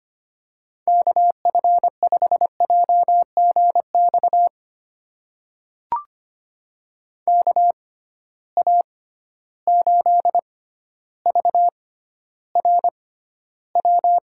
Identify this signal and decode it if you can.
Morse code, heard as a keyed tone, about 25 wpm, spaced as for 13 wpm, 700 Hz. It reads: KF5JGX KA8VRW
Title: USA Call Signs - 6 Characters - 6x Character Spacing - 25wpm